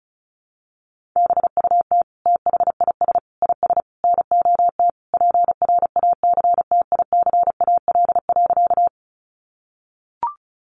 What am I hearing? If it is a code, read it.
Morse code, heard as a keyed tone, about 34 wpm, 700 Hz. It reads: BUT THIS IS NOT PRACTICAL.
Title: Sentences from Top 1400 Words 35wpm